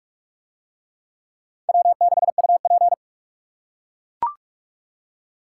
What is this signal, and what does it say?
Morse code, heard as a keyed tone, about 41 wpm, 700 Hz. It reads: W6UP